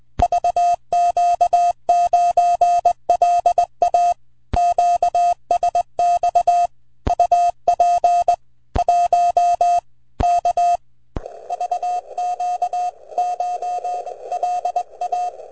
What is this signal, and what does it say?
Morse code, heard as a keyed tone, about 20 wpm, 710 Hz. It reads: VQ9LA QSX UP 1 K VQCH LA
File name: vq9la / CX5BW 80MTR CW 01SEP2008 0114Z